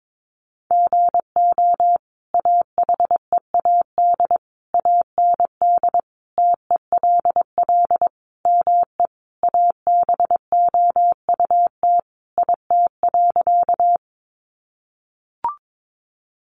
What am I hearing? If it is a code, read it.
Morse code, heard as a keyed tone, about 22 wpm, 700 Hz. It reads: GO AHEAD AND TELL ME ABOUT IT.
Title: Sentences from Top 1100 Words 22wpm